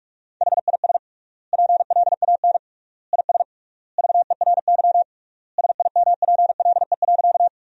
Morse code, heard as a keyed tone, about 46 wpm, 700 Hz. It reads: HIS PLAN IS VERY SIMPLE.
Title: 3x Word-Spacing - Sentences from Top 500 Words - 45wpm